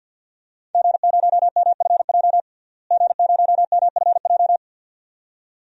Morse code, heard as a keyed tone, about 50 wpm, 700 Hz. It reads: G0MFJ G0MFJ
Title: DXCC-100 Call Signs - 5 Characters - Repeated 2x - 50wpm